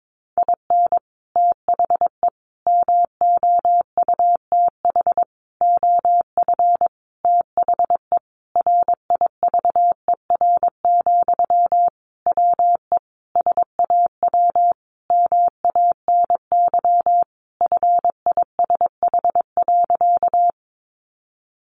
Morse code, heard as a keyed tone, about 22 wpm, 700 Hz. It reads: IN THE MOUTH OF THE RIVER, WE SAW MANY FISH.
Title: Sentences from Top 1500 Words 22wpm